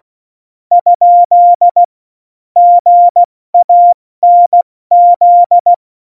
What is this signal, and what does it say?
Morse code, heard as a keyed tone, about 16 wpm, spaced as for 14 wpm, 700 Hz. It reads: ? GANZ
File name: Morsekurs / Witze / Witz 03.Dezember 2019, 70 F80